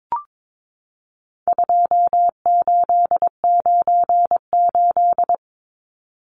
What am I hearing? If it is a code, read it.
Morse code, heard as a keyed tone, about 22 wpm, 700 Hz. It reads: 2898
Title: Numbers - 4-Digit 22wpm